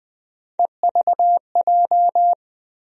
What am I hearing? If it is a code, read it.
Morse code, heard as a keyed tone, about 20 wpm, 700 Hz. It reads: EVJ
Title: USA Call Sign Postfixes - 3 Letters 20wpm